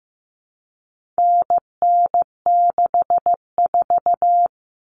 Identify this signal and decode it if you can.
Morse code, heard as a keyed tone, about 15 wpm, 700 Hz. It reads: NN64
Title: Common Cut Numbers - 4-Digit 15wpm